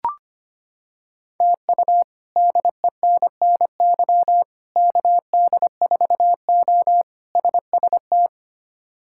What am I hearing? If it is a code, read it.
Morse code, heard as a keyed tone, about 25 wpm, 700 Hz. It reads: TU DENNY KD4O SST
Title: K1USN SST QSO Practice - 25wpm